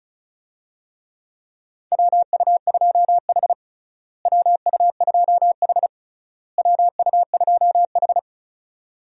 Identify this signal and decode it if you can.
Morse code, heard as a keyed tone, about 34 wpm, 700 Hz. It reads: WU2H WU2H WU2H
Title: USA Call Signs - 4 Characters - Repeated 3x - 35wpm